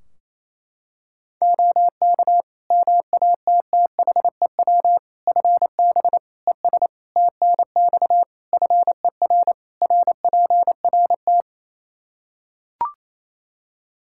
Morse code, heard as a keyed tone, about 28 wpm, 700 Hz. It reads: OK MATTHEW FB ES TNX FER RPRT
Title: Practice QSO - Exchange 2 - Info - 28wpm